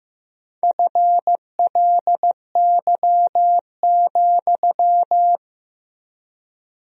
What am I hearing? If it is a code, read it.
Morse code, heard as a keyed tone, about 15 wpm, 700 Hz. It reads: FLY,
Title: 4x Word-Spacing - Sentences from Top 500 Words - 15wpm